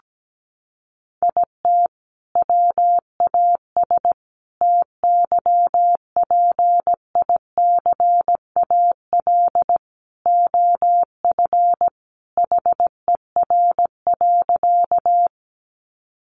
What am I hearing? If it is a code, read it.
Morse code, heard as a keyed tone, about 17 wpm, 700 Hz. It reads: IT WAS TYPICAL OF HER.
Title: Sentences from Top 1100 Words 17wpm